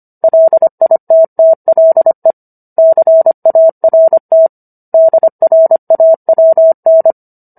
Morse code, heard as a keyed tone, about 25 wpm, 650 Hz. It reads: LITTLE CART DRAWN